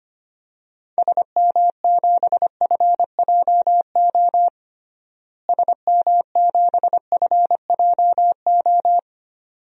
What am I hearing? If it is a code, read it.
Morse code, heard as a keyed tone, about 25 wpm, 700 Hz. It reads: SM7FJO SM7FJO
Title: DXCC-100 Call Signs - 6 Characters - Repeated 2x - 25wpm